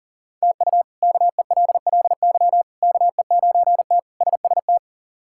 Morse code, heard as a keyed tone, about 40 wpm, 700 Hz. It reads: TU KELLY KE9T SST
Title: K1USN SST QSO Practice - 40wpm